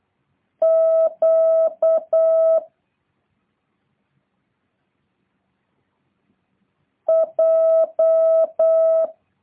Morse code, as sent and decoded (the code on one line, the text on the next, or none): --.- / .---
Q J